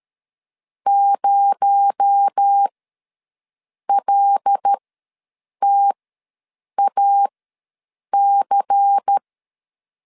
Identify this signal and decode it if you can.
Morse code, heard as a keyed tone, about 13 wpm, spaced as for 7 wpm, 800 Hz. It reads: 0LTAC
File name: 7 GF9Z ws 12wpm